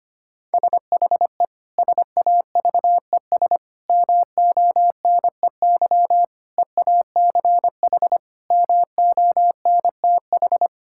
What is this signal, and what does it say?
Morse code, heard as a keyed tone, about 25 wpm, 700 Hz. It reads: SHE SAVES MONEY EACH MONTH
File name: practice / 25wpm / Difficult Sentences with Hints - 25wpm